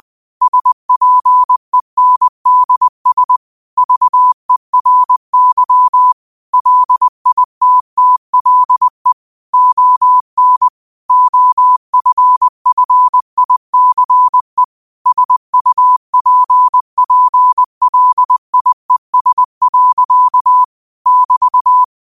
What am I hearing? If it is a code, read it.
Morse code, heard as a keyed tone, about 20 wpm, 1000 Hz. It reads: SPENDS VERY LITTLE ON OFFICE SUPPLIES. <BT>
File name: qotdfiles / 20wpm